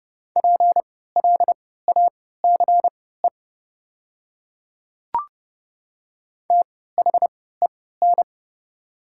Morse code, heard as a keyed tone, about 30 wpm, spaced as for 17 wpm, 700 Hz. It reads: PLACE THEN
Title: 3x Character Spacing - Top 300 Words - 30wpm